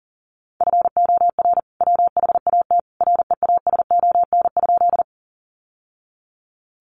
Morse code, heard as a keyed tone, about 40 wpm, 700 Hz. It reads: FOR WHAT REASON?
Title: Sentences from Top 700 Words 40wpm